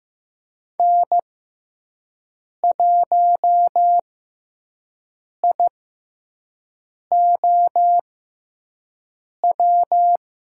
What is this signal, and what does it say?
Morse code, heard as a keyed tone, about 15 wpm, spaced as for 5 wpm, 700 Hz. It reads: N1IOW
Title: USA Call Signs - 6x Character Spacing - 15wpm